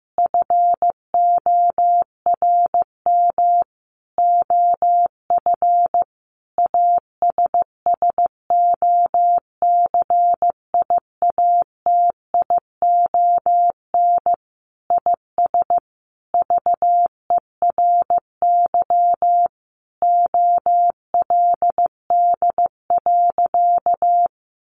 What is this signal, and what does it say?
Morse code, heard as a keyed tone, about 15 wpm, 700 Hz. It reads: FORM OF ASSOCIATION IS VERY OLD.
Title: Sentences from Top 1100 Words 15wpm